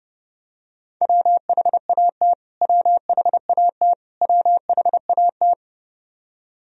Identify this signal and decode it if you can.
Morse code, heard as a keyed tone, about 30 wpm, 700 Hz. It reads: WHAT WHAT WHAT
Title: Top 100 Words Repeated 30wpm